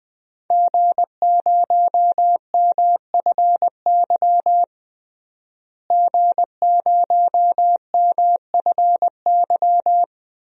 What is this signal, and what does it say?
Morse code, heard as a keyed tone, about 21 wpm, 700 Hz. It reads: G0MFY G0MFY
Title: DXCC-100 Call Signs - 5 Characters - Repeated 3x - 20wpm